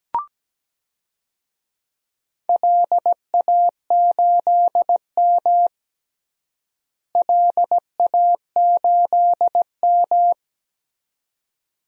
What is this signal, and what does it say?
Morse code, heard as a keyed tone, about 17 wpm, 700 Hz. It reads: LA8M LA8M
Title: DXCC-100 Call Signs - 4 Characters - Repeated 2x - 17wpm